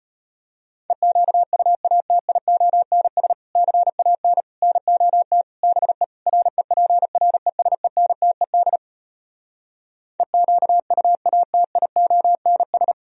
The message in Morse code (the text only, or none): EQUATIONS CAN NOT BE REPRESENTED EQUATIONS